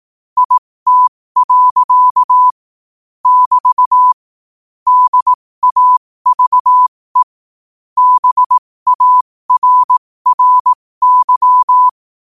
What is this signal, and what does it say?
Morse code, heard as a keyed tone, about 18 wpm, spaced as for 15 wpm, 1000 Hz. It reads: IT. <BT> DAVE BARRY
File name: qotdfiles / 15wpm